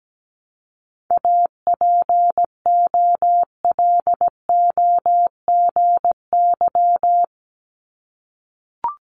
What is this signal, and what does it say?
Morse code, heard as a keyed tone, about 17 wpm, 700 Hz. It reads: APOLOGY
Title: Multi-letter Phonemes 17wpm